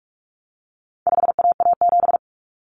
Morse code, heard as a keyed tone, about 44 wpm, 700 Hz. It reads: 5AA7
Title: Common Cut Numbers - 4-Digit 45wpm